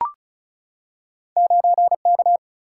Morse code, heard as a keyed tone, about 36 wpm, 700 Hz. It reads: 9K
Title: Single-Letter Single-Number 35wpm